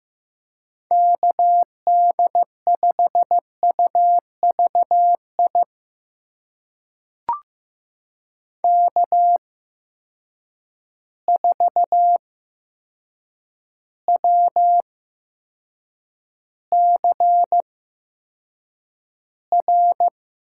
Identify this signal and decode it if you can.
Morse code, heard as a keyed tone, about 15 wpm, spaced as for 7 wpm, 700 Hz. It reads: KD5UVI K4WCR
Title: USA Call Signs - 8x Character Spacing - 15wpm